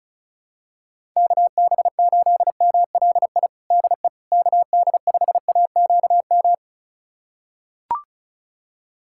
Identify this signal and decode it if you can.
Morse code, heard as a keyed tone, about 36 wpm, 700 Hz. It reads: KB8MLI DE KD5AQM
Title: USA Callsign DE Callsign 35wpm